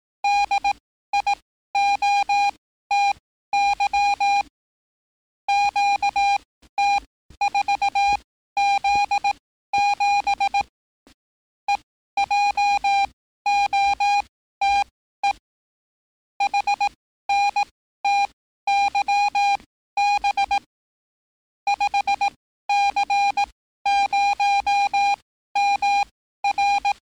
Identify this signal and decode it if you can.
Morse code, heard as a keyed tone, about 18 wpm, spaced as for 13 wpm, 810 Hz. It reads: DIOTY QT4Z7 EJOTE HNTYB 5C0MR
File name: training / NAVPERS 11049to11060 / NAVPERS 11054 RA RB